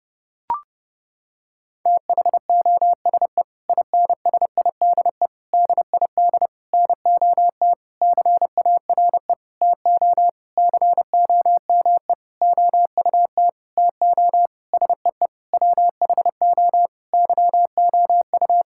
Morse code, heard as a keyed tone, about 30 wpm, 700 Hz. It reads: THOSE INSIDE DID NOT CARE TO COME OUT TO SEE WHO YOU
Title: Sentences from Top 1000 Words - Review Part 6 - 30wpm